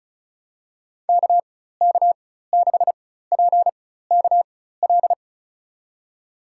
Morse code, heard as a keyed tone, about 36 wpm, spaced as for 16 wpm, 700 Hz. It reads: KK6PKL
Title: USA Call Signs - 6 Characters - 4x Character Spacing - 35wpm